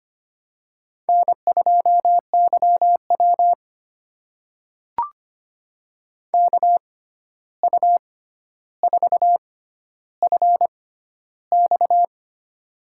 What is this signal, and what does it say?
Morse code, heard as a keyed tone, about 25 wpm, spaced as for 12 wpm, 700 Hz. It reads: N2YW KU4FX
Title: USA Call Signs - 6x Character Spacing - 25wpm